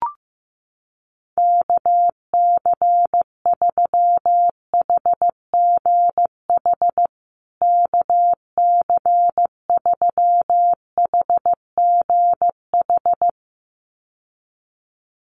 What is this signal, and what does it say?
Morse code, heard as a keyed tone, about 15 wpm, 700 Hz. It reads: KC3HGH KC3HGH
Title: USA Call Signs Repeated 2x 15wpm